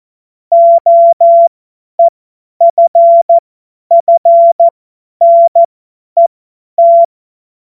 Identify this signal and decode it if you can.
Morse code, heard as a keyed tone, about 14 wpm, spaced as for 10 wpm, 680 Hz. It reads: OEFFNET
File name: Morsekurs / Witze / Witz 13 Februar 2024 50 F70